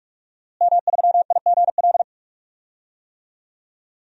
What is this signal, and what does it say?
Morse code, heard as a keyed tone, about 47 wpm, 700 Hz. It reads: M3IGL
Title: DXCC-100 Call Signs - 5 Characters - Repeated 2x - 45wpm